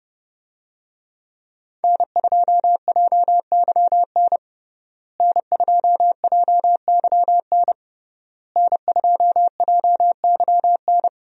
Morse code, heard as a keyed tone, about 30 wpm, 700 Hz. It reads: N2JYN N2JYN N2JYN
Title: USA Call Signs - 5 Characters - Repeated 3x - 30wpm